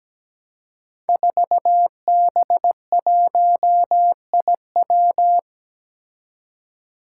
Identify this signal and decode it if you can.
Morse code, heard as a keyed tone, about 17 wpm, 700 Hz. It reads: 4B1IW
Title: Five Challenging Letter-Number Combinations 17wpm